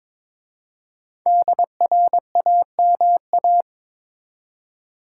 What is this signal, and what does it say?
Morse code, heard as a keyed tone, about 22 wpm, 700 Hz. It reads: DRAMA
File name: 5 Letter Words - First 3 Letters Given - 22wpm